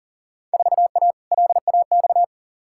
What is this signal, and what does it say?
Morse code, heard as a keyed tone, about 40 wpm, 700 Hz. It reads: 4A LAX
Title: Field Day Exchanges - 40wpm